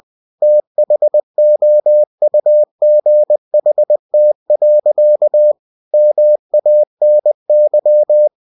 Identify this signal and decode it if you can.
Morse code, heard as a keyed tone, about 20 wpm, 600 Hz. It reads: THOUGHT. MANY